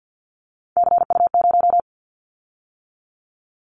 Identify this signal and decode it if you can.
Morse code, heard as a keyed tone, about 50 wpm, 700 Hz. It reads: CU0
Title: Callsign Prefixes for Top 50 DXCC - 50wpm